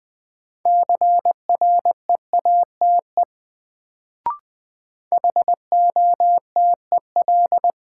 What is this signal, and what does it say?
Morse code, heard as a keyed tone, about 21 wpm, 700 Hz. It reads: CREATE HOTEL
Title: Sentences from Top 1600 Words - New Word Drill Alt - 20wpm